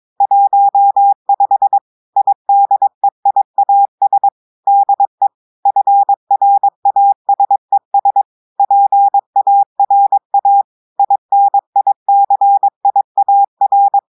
Morse code, heard as a keyed tone, about 22 wpm, 800 Hz. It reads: 15 IDEIAS DE FRASES PARA INICIAR
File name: audios / portugais / portugais texte en clair 2 vitesse 22 mots minutes